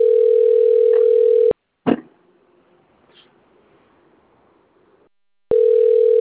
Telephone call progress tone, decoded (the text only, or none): ringback tone